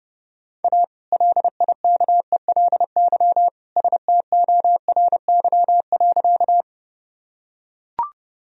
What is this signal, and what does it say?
Morse code, heard as a keyed tone, about 30 wpm, 700 Hz. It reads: A LIKELY STORY.
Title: Sentences from Top 1000 Words - Review Part 4 - 30wpm